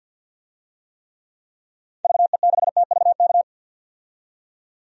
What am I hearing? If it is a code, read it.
Morse code, heard as a keyed tone, about 46 wpm, 700 Hz. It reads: VE6TVK
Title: DXCC-100 Call Signs - 6 Characters - 50wpm